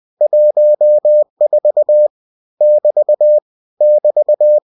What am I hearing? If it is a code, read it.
Morse code, heard as a keyed tone, about 20 wpm, 600 Hz. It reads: 14 <BT> <BT>
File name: cw-books / winnie-the-pooh / winnie-20wpm-page-0014